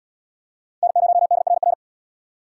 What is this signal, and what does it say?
Morse code, heard as a keyed tone, about 77 wpm, 700 Hz. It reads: N0DRR